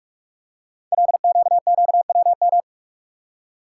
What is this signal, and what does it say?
Morse code, heard as a keyed tone, about 44 wpm, 690 Hz. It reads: LQQWM
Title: Code Groups - 5 Letters - 45wpm